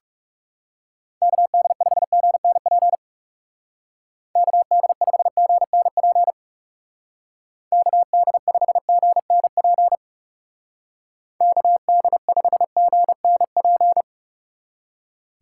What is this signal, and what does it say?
Morse code, heard as a keyed tone, about 36 wpm, 700 Hz. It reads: KD5GNP KD5GNP KD5GNP KD5GNP
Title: Speed Racing - USA 2x3 Call Signs 30wpm